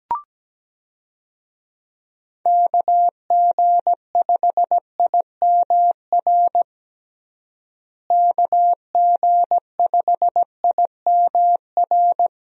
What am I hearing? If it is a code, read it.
Morse code, heard as a keyed tone, about 17 wpm, 700 Hz. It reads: KG5IMR KG5IMR
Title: USA Call Signs - 6 Characters - Repeated 2x - 17wpm